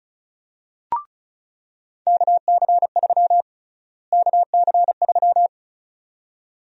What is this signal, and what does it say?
Morse code, heard as a keyed tone, about 36 wpm, 700 Hz. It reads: KC3 KC3
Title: USA Call Sign Prefixes - Repeated 2x - 35wpm